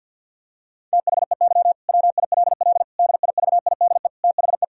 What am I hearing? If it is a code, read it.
Morse code, heard as a keyed tone, about 50 wpm, 700 Hz. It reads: THEY WILL DIVIDE THE
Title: Sentences from Top 1600 Words 50wpm